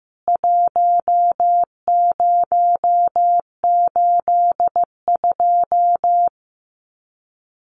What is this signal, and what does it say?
Morse code, heard as a keyed tone, about 14 wpm, 700 Hz. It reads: 1082
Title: CWT Exchange 15wpm